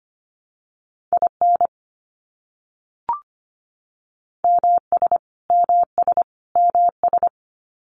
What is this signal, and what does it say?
Morse code, heard as a keyed tone, about 25 wpm, 700 Hz. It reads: IN MS MS MS
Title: US States Repeated 3x 25wpm